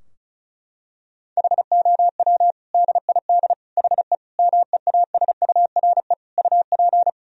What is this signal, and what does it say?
Morse code, heard as a keyed tone, about 35 wpm, 700 Hz. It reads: HOW DID HE MEASURE UP
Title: Sentences from Top 1000 Words - Review Part 2 - 35wpm